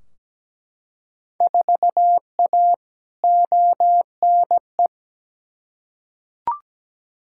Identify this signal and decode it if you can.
Morse code, heard as a keyed tone, about 17 wpm, 700 Hz. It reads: 4A ONE
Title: Field Day Exchanges Encore - 17wpm